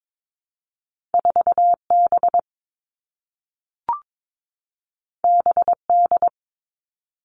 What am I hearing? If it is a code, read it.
Morse code, heard as a keyed tone, about 22 wpm, 700 Hz. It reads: 4 B B D
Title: Two Challenging Letter-Number Combinations 22wpm